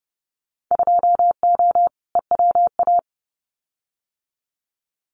Morse code, heard as a keyed tone, about 30 wpm, 700 Hz. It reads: 2O EWA
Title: Winter Field Day Exchanges Encore 30wpm